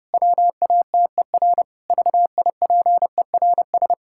Morse code, heard as a keyed tone, about 30 wpm, 700 Hz. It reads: WATER VIPERS
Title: Pangram Challenge v2 - 30wpm